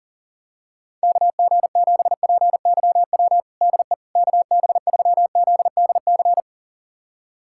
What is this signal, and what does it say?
Morse code, heard as a keyed tone, about 40 wpm, 700 Hz. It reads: KG7PYW DE KB3ZDC
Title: USA Callsign DE Callsign - 6 Characters - 40wpm